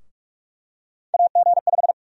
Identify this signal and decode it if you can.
Morse code, heard as a keyed tone, about 48 wpm, 700 Hz. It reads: AG5
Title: USA Call Sign Prefixes - 45wpm